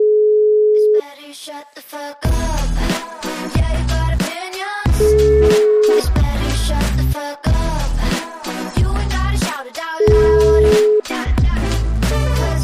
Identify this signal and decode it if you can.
Ringback tone (Europe, 425 Hz), with noise.